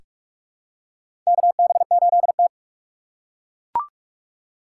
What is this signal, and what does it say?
Morse code, heard as a keyed tone, about 42 wpm, 700 Hz. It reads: KB8T
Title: USA Call Signs - 2x Character Spacing - 45wpm